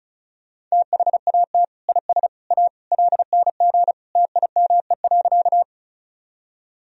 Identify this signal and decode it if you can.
Morse code, heard as a keyed tone, about 36 wpm, 700 Hz. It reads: THAT IS A LNG TIME.
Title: Sentences from Top 100 Words and Abbreviations 35wpm